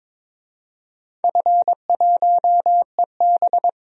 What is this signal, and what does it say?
Morse code, heard as a keyed tone, about 22 wpm, 700 Hz. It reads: F1EB
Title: DXCC-100 Call Signs - 4 Characters - 22wpm